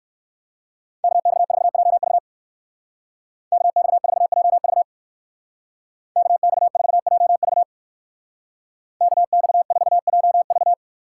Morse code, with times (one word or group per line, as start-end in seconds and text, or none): 1.0-2.2 KX4JV
3.5-4.8 KX4JV
6.2-7.6 KX4JV
9.0-10.7 KX4JV